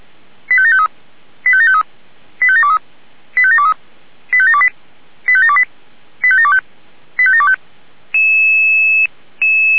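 Five-tone selective call (ZVEI), heard as quick runs of five tones, tones of about 80 ms. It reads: 85753 85753 85723 85723 85728 85728 85725 85725
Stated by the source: ZVEI five-tone selective call received on SDR.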